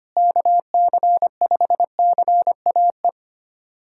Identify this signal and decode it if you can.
Morse code, heard as a keyed tone, about 25 wpm, 700 Hz. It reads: KC5CAE
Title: USA Callsign DE Callsign - 6 Characters - 25wpm